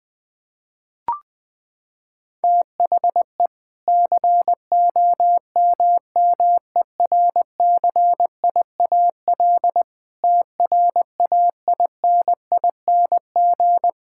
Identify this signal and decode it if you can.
Morse code, heard as a keyed tone, about 21 wpm, 700 Hz. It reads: THE COMMERCIAL TRAINING
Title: Sentences from Top 1000 Words - Review Part 5 - 20wpm